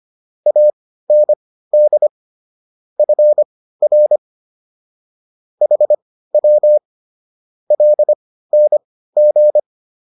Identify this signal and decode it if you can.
Morse code, heard as a keyed tone, about 25 wpm, spaced as for 15 wpm, 600 Hz. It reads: AND FR HW LNG